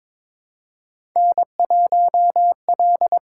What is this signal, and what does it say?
Morse code, heard as a keyed tone, about 21 wpm, 700 Hz. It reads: N1L
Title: USA Call Signs - 4 Characters - 6x Character Spacing - 22wpm